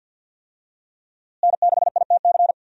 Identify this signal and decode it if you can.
Morse code, heard as a keyed tone, about 51 wpm, 700 Hz. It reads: N6ITC
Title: USA Call Signs - 5 Characters - Repeated 2x - 50wpm